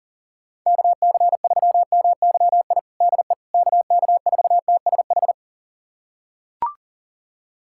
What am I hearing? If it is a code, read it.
Morse code, heard as a keyed tone, about 40 wpm, 700 Hz. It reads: KC3MYI DE KK4TSH
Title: USA Callsign DE Callsign - 6 Characters - 40wpm